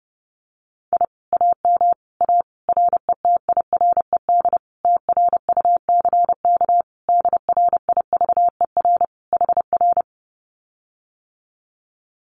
Morse code, heard as a keyed tone, about 30 wpm, 700 Hz. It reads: I AM A RETIRED TRUCK DRIVER HR
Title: Common QSO Phrases Encore 30wpm